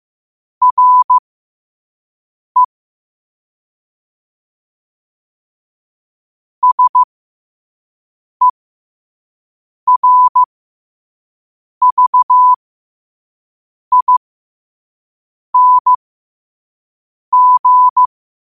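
Morse code, heard as a keyed tone, about 15 wpm, spaced as for 5 wpm, 1000 Hz. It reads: RE SERVING
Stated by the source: Morse code sent at 5 wpm for learning.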